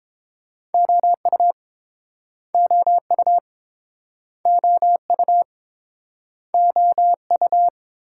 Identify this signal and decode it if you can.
Morse code, heard as a keyed tone, about 30 wpm, 700 Hz. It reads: OU OU OU OU